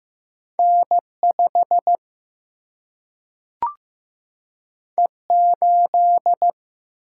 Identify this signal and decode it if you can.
Morse code, heard as a keyed tone, about 15 wpm, 700 Hz. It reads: N5 E8
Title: Single-Letter Single-Number 15wpm